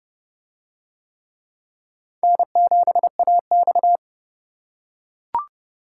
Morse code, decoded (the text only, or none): N7AX